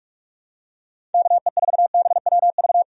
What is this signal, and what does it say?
Morse code, heard as a keyed tone, about 46 wpm, 700 Hz. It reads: KE4BWV
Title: USA Call Signs - 6 Characters - 45wpm